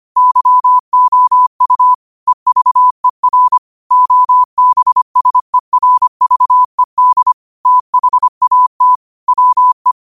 Morse code, heard as a keyed tone, about 25 wpm, 1000 Hz. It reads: YOU EVER OBSERVED THAT WE
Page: Quotes for Wed, 13 Aug 2025 in Morse Code at 25 words per minute.